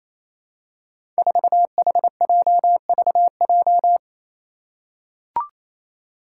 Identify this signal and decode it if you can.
Morse code, heard as a keyed tone, about 27 wpm, 700 Hz. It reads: 4HJVJ
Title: Five Challenging Letter-Number Combinations 28wpm